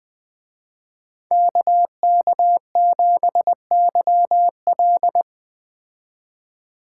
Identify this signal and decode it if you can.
Morse code, heard as a keyed tone, about 20 wpm, 700 Hz. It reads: KK7YL
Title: Speed Racing - USA 2x2 Call Signs 17wpm